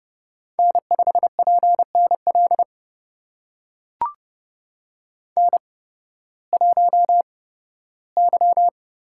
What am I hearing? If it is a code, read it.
Morse code, heard as a keyed tone, about 30 wpm, spaced as for 16 wpm, 700 Hz. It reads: N5PNL N1Y
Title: USA Call Signs - 5 Characters - 8x Character Spacing - 30wpm